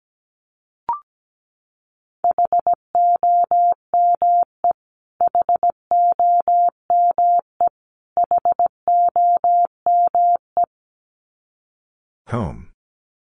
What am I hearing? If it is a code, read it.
Morse code, heard as a keyed tone, about 17 wpm, 700 Hz. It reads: HOME HOME HOME
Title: Top 500 Words Repeated 17wpm